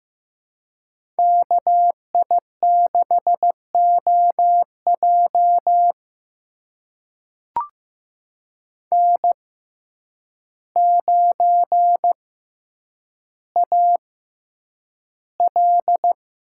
Morse code, heard as a keyed tone, about 15 wpm, spaced as for 8 wpm, 700 Hz. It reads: KI6OJ N9AL
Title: USA Call Signs - 5 Characters - 6x Character Spacing - 15wpm